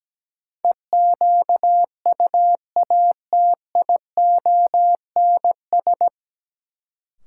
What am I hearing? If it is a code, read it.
Morse code, heard as a keyed tone, about 17 wpm, 700 Hz. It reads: EQUATIONS
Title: Sets of 5 Words - New Word Drill - 17wpm